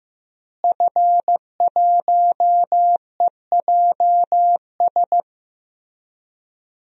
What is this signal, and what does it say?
Morse code, heard as a keyed tone, about 15 wpm, 700 Hz. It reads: F1EJS
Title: DXCC-100 Call Signs - 5 Characters - Repeated 3x - 15wpm